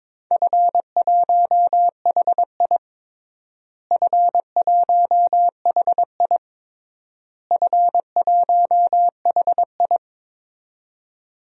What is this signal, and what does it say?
Morse code, heard as a keyed tone, about 22 wpm, 700 Hz. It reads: F1HI F1HI F1HI
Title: DXCC-100 Call Signs - 4 Characters - Repeated 3x - 22wpm